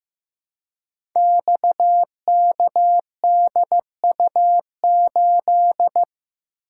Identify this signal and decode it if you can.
Morse code, heard as a keyed tone, about 15 wpm, 700 Hz. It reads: XKDU8
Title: Code Groups - 5 Letter:Numbers - 15wpm